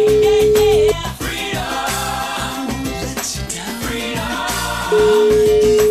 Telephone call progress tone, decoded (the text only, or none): ringback tone